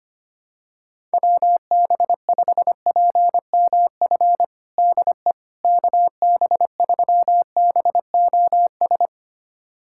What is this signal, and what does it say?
Morse code, heard as a keyed tone, about 25 wpm, 700 Hz. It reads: WB5PMF DE KB3BOS
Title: USA Callsign DE Callsign 25wpm